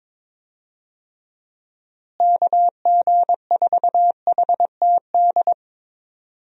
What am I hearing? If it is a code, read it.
Morse code, heard as a keyed tone, about 22 wpm, 700 Hz. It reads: KG4HTD
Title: USA Call Signs - 6 Characters - Repeated 2x - 22wpm